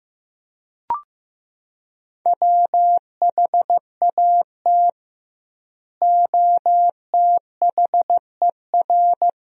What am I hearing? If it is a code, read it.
Morse code, heard as a keyed tone, about 15 wpm, 700 Hz. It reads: WHAT OTHER
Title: 2x Word-Spacing - Sentences from Top 100 Words - 15wpm